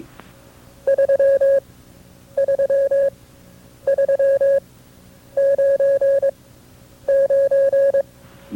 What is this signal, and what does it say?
Morse code, heard as a keyed tone, about 22 wpm, spaced as for 9 wpm, 550 Hz. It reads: 33399
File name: FreeFiles / CW-practice / KB6MT-tape2-side2